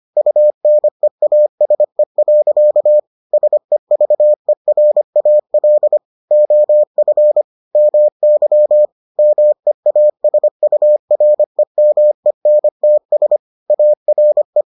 Morse code, heard as a keyed tone, about 25 wpm, 600 Hz. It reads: UNEASE. SEVERAL OF MY MEASUREMENTS ARE